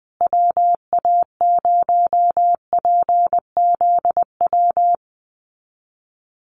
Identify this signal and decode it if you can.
Morse code, heard as a keyed tone, about 20 wpm, 700 Hz. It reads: WA0PZW
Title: USA Call Signs Repeated 3x 20wpm